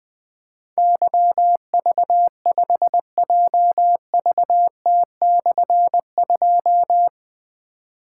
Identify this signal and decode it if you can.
Morse code, heard as a keyed tone, about 20 wpm, 700 Hz. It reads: YV5JVT/2
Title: Difficult Call Signs 20wpm